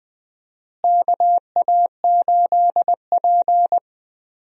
Morse code, heard as a keyed tone, about 20 wpm, 700 Hz. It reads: KA8P
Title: USA Call Signs - 4 Characters - Repeated 2x - 20wpm